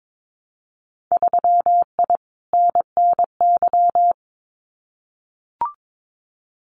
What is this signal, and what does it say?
Morse code, heard as a keyed tone, about 22 wpm, 700 Hz. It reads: 3I NNY
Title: Winter Field Day Exchanges 22wpm